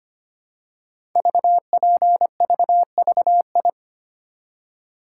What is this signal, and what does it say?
Morse code, heard as a keyed tone, about 25 wpm, 700 Hz. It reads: VPVVI
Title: Code Groups - 5 Letters - 25wpm